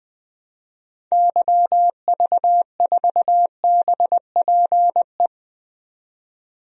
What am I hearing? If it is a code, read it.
Morse code, heard as a keyed tone, about 21 wpm, 700 Hz. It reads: YV4BPE
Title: DXCC-100 Call Signs - 6 Characters - 20wpm